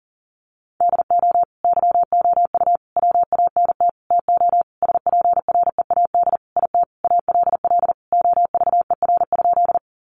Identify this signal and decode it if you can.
Morse code, heard as a keyed tone, about 40 wpm, 700 Hz. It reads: DO YOU WANT TO SPREAD IT ALL OVER?
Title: Sentences from Top 1200 Words 40wpm